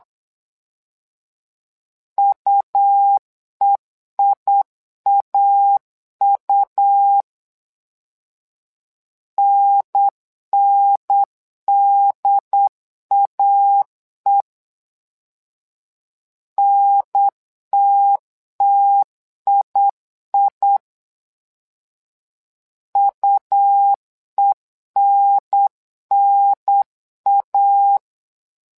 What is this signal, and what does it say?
Morse code, heard as a keyed tone, about 8 wpm, 800 Hz. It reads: UEIAU NNDAE NTTII UENNA
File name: LEZIONI / 2 UHDN / 2 UHDN 8wpm2232